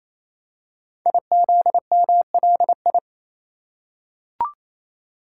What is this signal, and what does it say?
Morse code, heard as a keyed tone, about 27 wpm, 700 Hz. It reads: IZMLI
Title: Code Groups - 5 Letters - 28wpm